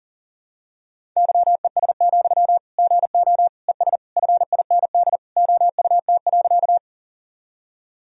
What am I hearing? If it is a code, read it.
Morse code, heard as a keyed tone, about 37 wpm, 700 Hz. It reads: YES, GO ES FIND OUT.
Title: Sentences from Top 500 Words and Abbreviations 40wpm